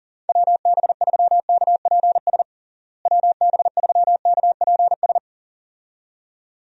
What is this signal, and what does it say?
Morse code, heard as a keyed tone, about 40 wpm, 700 Hz. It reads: WB3KPS WB3KPS
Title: USA Call Signs - 6 Characters - Repeated 2x - 40wpm